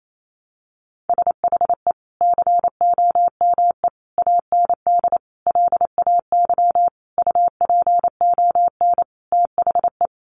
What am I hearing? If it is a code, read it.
Morse code, heard as a keyed tone, about 28 wpm, 700 Hz. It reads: SHE COME AND LAY UPON THE
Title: Sentences from Top 1100 Words 28wpm